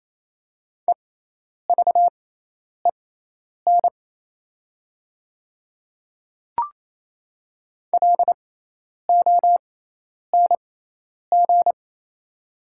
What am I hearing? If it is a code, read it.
Morse code, heard as a keyed tone, about 27 wpm, spaced as for 10 wpm, 690 Hz. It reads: EVEN LONG
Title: 6x Character Spacing - Top 300 Words - 28wpm